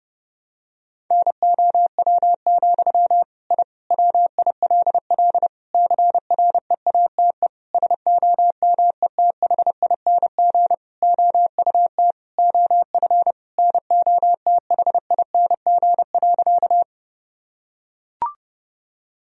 Morse code, heard as a keyed tone, about 30 wpm, 700 Hz. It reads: NOW, I WILL CREATE SOMETHING OUT OF NOTHING.
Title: Sentences from Top 1000 Words - Review Part 3 - 30wpm